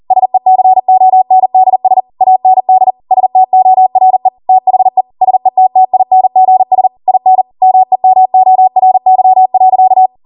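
Morse code, heard as a keyed tone, about 40 wpm, 750 Hz. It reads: SECONDS AND STORE THE SETTINGS IN MEMORY.
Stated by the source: Code practice at 40 wpm.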